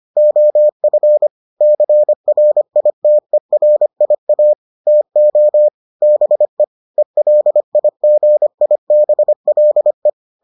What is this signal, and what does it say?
Morse code, heard as a keyed tone, about 25 wpm, 600 Hz. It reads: OF CRITERIA TO BE ELIGIBLE